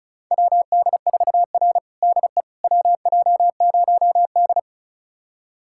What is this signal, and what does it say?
Morse code, heard as a keyed tone, about 34 wpm, 700 Hz. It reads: WD4R DE WJ0D
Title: USA Callsign DE Callsign - 4 Characters - 35wpm